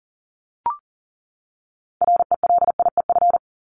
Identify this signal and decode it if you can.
Morse code, heard as a keyed tone, about 40 wpm, 700 Hz. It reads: RELIEF
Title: Top 1501-1750 Words 40wpm